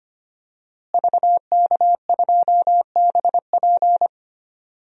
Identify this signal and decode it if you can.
Morse code, heard as a keyed tone, about 25 wpm, 700 Hz. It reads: VK2BP
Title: DXCC-100 Call Signs - 5 Characters - Repeated 2x - 25wpm